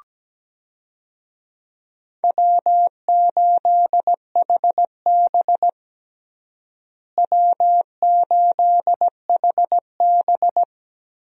Morse code, heard as a keyed tone, about 17 wpm, 700 Hz. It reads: W8HB W8HB
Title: USA Call Signs - 4 Characters - Repeated 3x - 17wpm